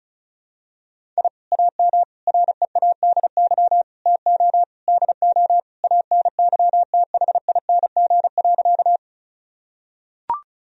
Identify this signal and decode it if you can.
Morse code, heard as a keyed tone, about 34 wpm, 700 Hz. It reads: I AM READY TO DO ANYTHING.
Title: Sentences from Top 1000 Words - Review Part 2 - 35wpm